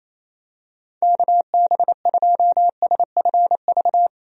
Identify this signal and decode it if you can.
Morse code, heard as a keyed tone, about 27 wpm, 700 Hz. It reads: KB2SFV
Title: USA Call Signs - 6 Characters - Repeated 2x - 28wpm